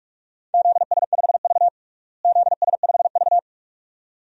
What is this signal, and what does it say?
Morse code, heard as a keyed tone, about 46 wpm, 700 Hz. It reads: ZS5V ZS5V
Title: DXCC-100 Call Signs - 4 Characters - Repeated 2x - 45wpm